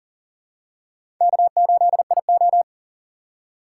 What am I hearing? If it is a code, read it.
Morse code, heard as a keyed tone, about 37 wpm, 700 Hz. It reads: K8IO
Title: USA Call Signs - 4 Characters - 6x Character Spacing - 40wpm